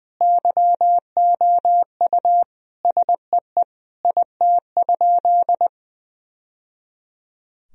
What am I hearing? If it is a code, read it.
Morse code, heard as a keyed tone, about 20 wpm, 700 Hz. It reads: YOU SEE IT?
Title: Sentences from Top 1000 Words - Review Part 4 - 20wpm